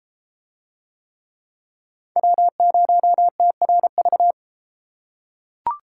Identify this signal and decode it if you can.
Morse code, heard as a keyed tone, about 32 wpm, 700 Hz. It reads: W0TRV